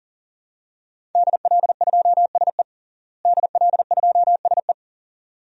Morse code, heard as a keyed tone, about 37 wpm, 700 Hz. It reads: DL2SE DL2SE
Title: DXCC-100 Call Signs - 5 Characters - Repeated 2x - 40wpm